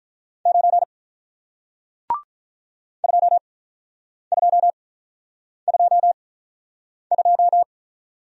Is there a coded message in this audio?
Yes: Morse code from 0.5 to 7.6 s, heard as a keyed tone, about 48 wpm, spaced as for 9 wpm, 700 Hz. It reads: 9 2222